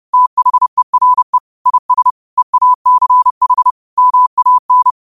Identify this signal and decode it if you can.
Morse code, heard as a keyed tone, about 30 wpm, 1000 Hz. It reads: THERE IS EACH MAN